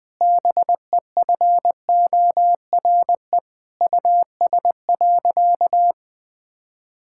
Morse code, heard as a keyed tone, about 20 wpm, 700 Hz. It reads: BEFORE US.
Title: Sentences from Top 1000 Words - Review Part 5 - 20wpm